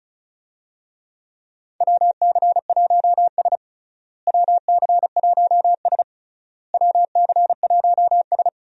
Morse code, heard as a keyed tone, about 34 wpm, 700 Hz. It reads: WC1S WC1S WC1S
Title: USA Call Signs - 4 Characters - Repeated 3x - 35wpm